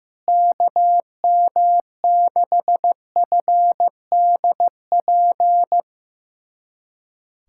Morse code, heard as a keyed tone, about 15 wpm, 700 Hz. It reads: KM6FDP
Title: USA Callsign DE Callsign - 6 Characters - 15wpm